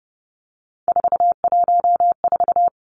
Morse code, heard as a keyed tone, about 30 wpm, 700 Hz. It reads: 414
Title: Numbers - 3-Digit 30wpm